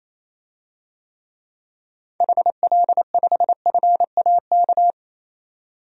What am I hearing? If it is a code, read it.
Morse code, heard as a keyed tone, about 28 wpm, 700 Hz. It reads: HL5FAK
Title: DXCC-100 Call Signs - 6 Characters - 28wpm